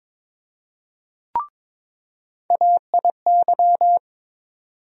Morse code, heard as a keyed tone, about 22 wpm, 700 Hz. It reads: AIY
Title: USA Call Sign Postfixes - 3 Letters 22wpm